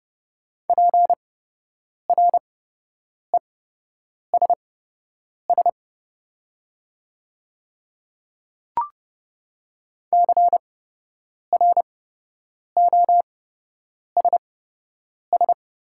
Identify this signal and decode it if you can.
Morse code, heard as a keyed tone, about 28 wpm, spaced as for 8 wpm, 700 Hz. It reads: PRESS CROSS
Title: 8x Character Spacing - Top 500 Words - 30wpm